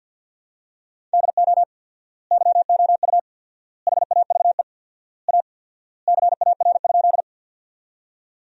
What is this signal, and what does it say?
Morse code, heard as a keyed tone, about 52 wpm, 700 Hz. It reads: DO YOU HAVE A CAR?
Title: 4x Word-Spacing - Sentences from Top 300 Words - 50wpm